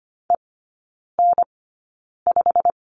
Morse code, heard as a keyed tone, about 25 wpm, spaced as for 9 wpm, 700 Hz. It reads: EN5
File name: Lesson 15e - Number 5 - Three Characters 25wpm